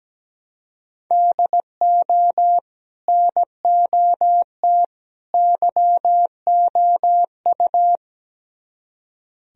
Morse code, heard as a keyed tone, about 18 wpm, 700 Hz. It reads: DO NOT YOU
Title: Speed-Racing - Sets of 3 Words 17wpm